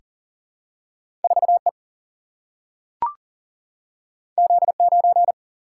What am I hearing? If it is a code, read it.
Morse code, heard as a keyed tone, about 40 wpm, 700 Hz. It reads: 4E Z9
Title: Single-Letter Single-Number 40wpm